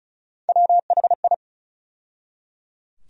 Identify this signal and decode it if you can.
Morse code, heard as a keyed tone, about 34 wpm, 700 Hz. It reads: WHI
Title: USA Call Sign Postfixes - 3 Letters 35wpm